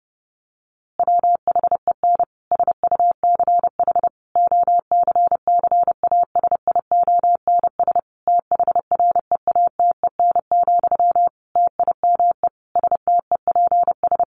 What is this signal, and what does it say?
Morse code, heard as a keyed tone, about 30 wpm, 700 Hz. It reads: WHEN SUCH OCCASIONS THREATEN, TIME STEPS
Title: A Sound Of Thunder 30wpm